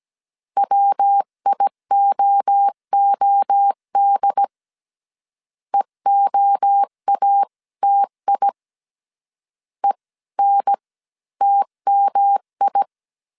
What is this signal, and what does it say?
Morse code, heard as a keyed tone, about 17 wpm, spaced as for 14 wpm, 800 Hz. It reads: WIOOD EOATI ENTMI